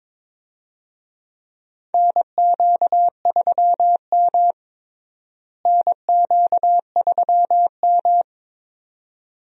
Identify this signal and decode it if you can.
Morse code, heard as a keyed tone, about 22 wpm, 700 Hz. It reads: NQ3M NQ3M
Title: USA Call Signs - 4 Characters - Repeated 2x - 22wpm